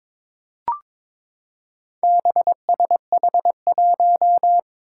Morse code, heard as a keyed tone, about 22 wpm, 700 Hz. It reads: BSH1
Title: Four Challenging Letter-Number Combinations 22wpm